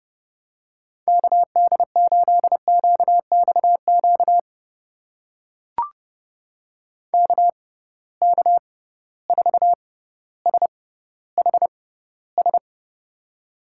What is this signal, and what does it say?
Morse code, heard as a keyed tone, about 30 wpm, spaced as for 15 wpm, 700 Hz. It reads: KD8QXQ KK4SHS
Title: USA Call Signs - 6 Characters - 6x Character Spacing - 30wpm